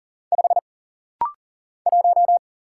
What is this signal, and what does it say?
Morse code, heard as a keyed tone, about 41 wpm, 700 Hz. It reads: 5 1
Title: Single Letter-Number - Rapid-Fire 40wpm